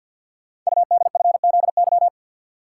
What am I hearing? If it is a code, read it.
Morse code, heard as a keyed tone, about 50 wpm, 700 Hz. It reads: UDFZY
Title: Code Groups - 5 Letters - 50wpm